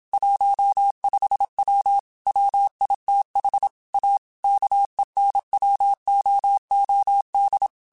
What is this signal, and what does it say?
Morse code, heard as a keyed tone, about 27 wpm, 800 Hz. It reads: 15W WITH A KENWOOD
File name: QSOs / QSOs-mp3s / WD0FBT 1@25